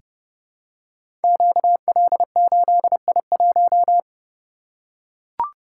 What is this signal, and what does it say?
Morse code, heard as a keyed tone, about 30 wpm, 700 Hz. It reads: QL8I1
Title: Code Groups - 5 Letter:Numbers - 30wpm